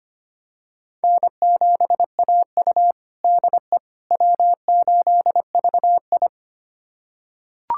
Morse code, heard as a keyed tone, about 25 wpm, 700 Hz. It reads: N7AU DE W8VI
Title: USA Callsign DE Callsign - 4 Characters - 25wpm